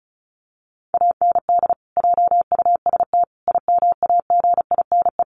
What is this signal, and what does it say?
Morse code, heard as a keyed tone, about 34 wpm, 700 Hz. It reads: AND JUST IMAGINE
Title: Sentences from Top 1400 Words 35wpm